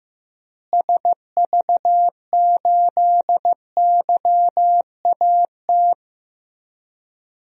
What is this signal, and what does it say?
Morse code, heard as a keyed tone, about 15 wpm, 700 Hz. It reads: SV8YAT
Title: DXCC-100 Call Signs - 6 Characters - Repeated 2x - 15wpm